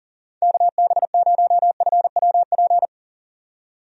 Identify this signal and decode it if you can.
Morse code, heard as a keyed tone, about 40 wpm, 700 Hz. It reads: KB0FWP
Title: USA Call Signs - 6 Characters - 40wpm